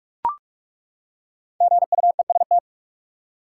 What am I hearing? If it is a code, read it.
Morse code, heard as a keyed tone, about 43 wpm, 700 Hz. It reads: GUEST
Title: Five Letter Words - 45wpm